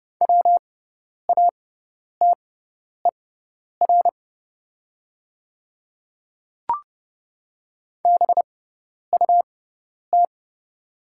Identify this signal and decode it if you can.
Morse code, heard as a keyed tone, about 30 wpm, spaced as for 10 wpm, 700 Hz. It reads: WATER BUT
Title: 6x Character Spacing - Top 200 Words - 30wpm